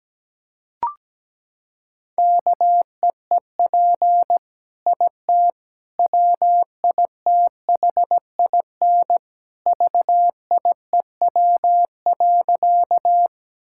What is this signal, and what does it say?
Morse code, heard as a keyed tone, about 17 wpm, 700 Hz. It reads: KEEP IT WITHIN VIEW.
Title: Sentences from Top 1000 Words - Review Part 6 - 17wpm